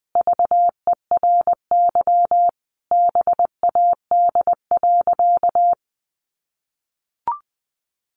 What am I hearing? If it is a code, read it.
Morse code, heard as a keyed tone, about 20 wpm, 700 Hz. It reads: VERY BAD.
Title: Sentences from Top 1200 Words 20wpm